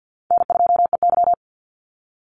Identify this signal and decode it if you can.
Morse code, heard as a keyed tone, about 50 wpm, 700 Hz. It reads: N2EY
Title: Call Signs Repeated 2x - 50wpm